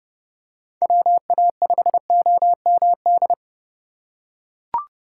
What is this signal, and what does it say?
Morse code, heard as a keyed tone, about 32 wpm, 700 Hz. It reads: WA5OMD
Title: USA Call Signs - 4x Character Spacing - 30wpm